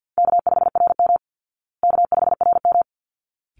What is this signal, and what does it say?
Morse code, heard as a keyed tone, about 50 wpm, 700 Hz. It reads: K5RM K5RM
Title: Easy Call Signs Repeated 2x - 50wpm